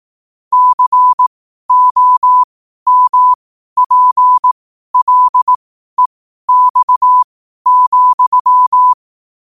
Morse code, heard as a keyed tone, about 18 wpm, spaced as for 12 wpm, 1000 Hz. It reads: COMPLEX,
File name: qotdfiles / 12wpm